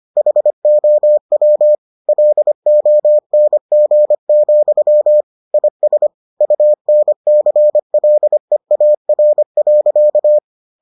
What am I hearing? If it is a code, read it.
Morse code, heard as a keyed tone, about 25 wpm, 600 Hz. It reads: HOW LONG, IS UNCLEAR.